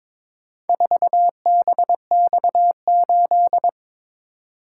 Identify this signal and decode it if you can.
Morse code, heard as a keyed tone, about 22 wpm, 700 Hz. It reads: 4BX8
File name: practice / 22wpm / Four Letter-Number Combinations - 22wpm